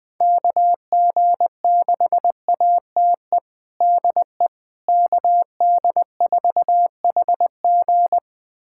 Morse code, heard as a keyed tone, about 21 wpm, 700 Hz. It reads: KG6ATE DE KD4HG
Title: USA Callsign DE Callsign 20wpm